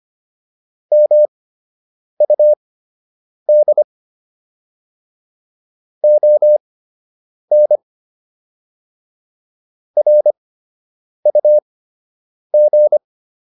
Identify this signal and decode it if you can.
Morse code, heard as a keyed tone, about 25 wpm, spaced as for 8 wpm, 600 Hz. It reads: MUD ON RUG